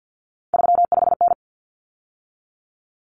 Morse code, heard as a keyed tone, about 50 wpm, 700 Hz. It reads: 35N
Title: Common Cut Numbers - 3-Digit 50wpm